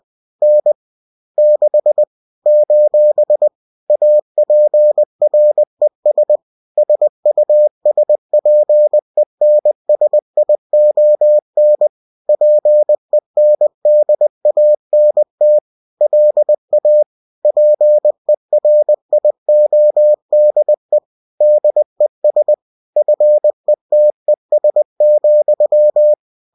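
Morse code, heard as a keyed tone, about 20 wpm, 600 Hz. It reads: N 6 : APRES SUSPENSION PENDANT LA PERIODE DES FETES,
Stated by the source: Bulletin sent in CW.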